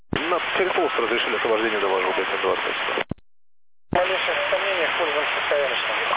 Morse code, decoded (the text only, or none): none